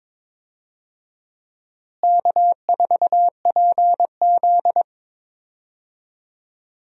Morse code, heard as a keyed tone, about 23 wpm, 700 Hz. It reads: K4PZ
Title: USA Call Signs - 4 Characters - Repeated 3x - 22wpm